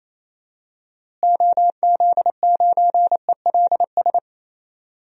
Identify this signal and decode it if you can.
Morse code, heard as a keyed tone, about 28 wpm, 700 Hz. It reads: OZ9ELS
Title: DXCC-100 Call Signs - 6 Characters - Repeated 2x - 28wpm